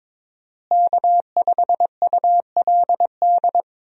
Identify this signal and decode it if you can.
Morse code, heard as a keyed tone, about 22 wpm, 700 Hz. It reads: K5ULD
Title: USA Call Signs - 5 Characters - Repeated 2x - 22wpm